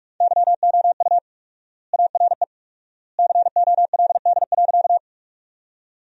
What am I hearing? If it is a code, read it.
Morse code, heard as a keyed tone, about 45 wpm, 700 Hz. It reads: YOU ARE COLD.
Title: 4x Word-Spacing - Sentences from Top 400 Words - 45wpm